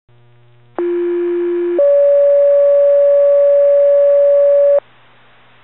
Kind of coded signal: two-tone page